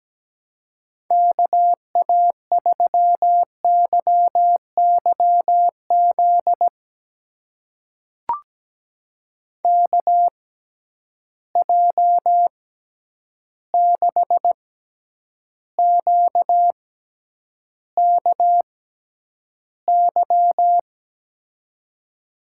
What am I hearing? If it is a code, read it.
Morse code, heard as a keyed tone, about 17 wpm, spaced as for 9 wpm, 700 Hz. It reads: KA3YYZ KJ6QKY
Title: USA Call Signs - 6 Characters - 6x Character Spacing - 17wpm